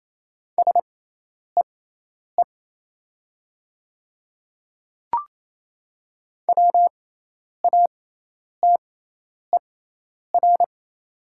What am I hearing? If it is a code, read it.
Morse code, heard as a keyed tone, about 28 wpm, spaced as for 10 wpm, 700 Hz. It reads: SEE WATER
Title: 6x Character Spacing - Top 100 Words - 28wpm